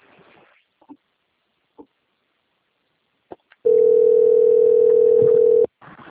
Ringback tone (North America), 440 + 480 Hz.